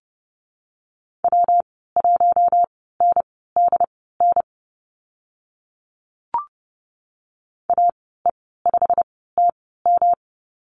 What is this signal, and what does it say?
Morse code, heard as a keyed tone, about 30 wpm, spaced as for 17 wpm, 700 Hz. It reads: W1NDN AE5TM
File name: Lesson 19g - Letter w - Call Signs 30wpm